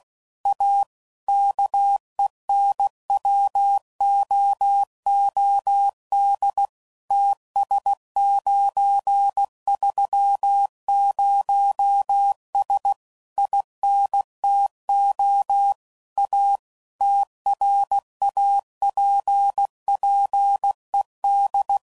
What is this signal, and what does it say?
Morse code, heard as a keyed tone, about 16 wpm, 800 Hz. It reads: A KENWOOD TS930S INTO A TRAPPED
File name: CW Code Groups @15WPM / QSOs / kc7sac